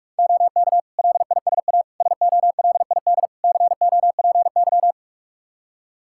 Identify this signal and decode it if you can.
Morse code, heard as a keyed tone, about 47 wpm, 700 Hz. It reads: OK LISA SOLID COPY
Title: Practice QSO - Exchange 3 - More Info - 45wpm